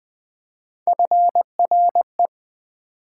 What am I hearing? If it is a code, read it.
Morse code, heard as a keyed tone, about 20 wpm, 700 Hz. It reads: FRE
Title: Build A Word - 3 Letters At A Time - 6-Letter Words - 20wpm